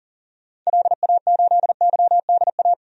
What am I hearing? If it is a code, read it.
Morse code, heard as a keyed tone, about 40 wpm, 700 Hz. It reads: LA8YDA
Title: DXCC-100 Call Signs - 6 Characters - 40wpm